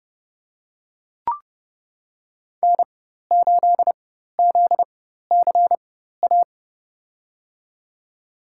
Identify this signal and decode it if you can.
Morse code, heard as a keyed tone, about 32 wpm, spaced as for 14 wpm, 700 Hz. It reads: N8ZCA
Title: USA Call Signs - 5 Characters - 4x Character Spacing - 30wpm